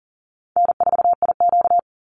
Morse code, heard as a keyed tone, about 40 wpm, 700 Hz. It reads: N4IQ
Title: Easy Call Signs Repeated 2x - 40wpm